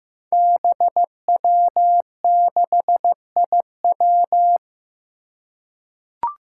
Morse code, heard as a keyed tone, about 15 wpm, 700 Hz. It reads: BW6IW
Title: Five Challenging Letter-Number Combinations 15wpm